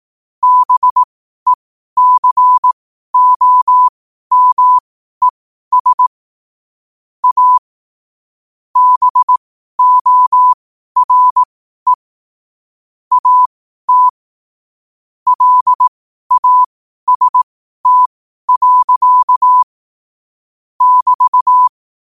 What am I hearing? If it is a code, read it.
Morse code, heard as a keyed tone, about 18 wpm, spaced as for 12 wpm, 1000 Hz. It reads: BECOMES A BORE AT LAST. <BT>
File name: qotdfiles / 12wpm